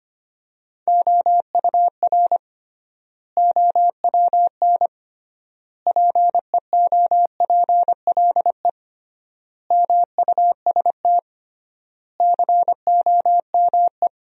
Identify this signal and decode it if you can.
Morse code, heard as a keyed tone, about 25 wpm, 700 Hz. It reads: OUR OWN PEOPLE MUST COME
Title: 3x Word-Spacing - Sentences from Top 300 Words - 25wpm